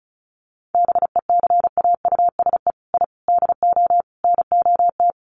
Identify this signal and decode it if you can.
Morse code, heard as a keyed tone, about 34 wpm, 700 Hz. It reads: BECAUSE I DO NOT
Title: Sets of 4 Words 35wpm